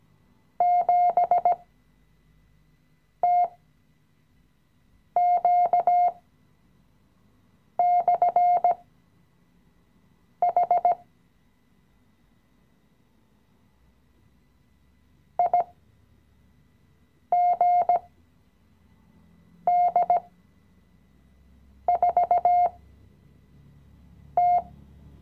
Morse code, heard as a keyed tone, about 16 wpm, spaced as for 5 wpm, 710 Hz. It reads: 7TQ/H IGD4T
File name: CW-Online-Kurs / Anfänger / Lektion 018